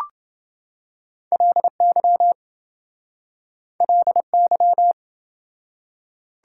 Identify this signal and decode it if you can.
Morse code, heard as a keyed tone, about 28 wpm, 700 Hz. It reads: LY LY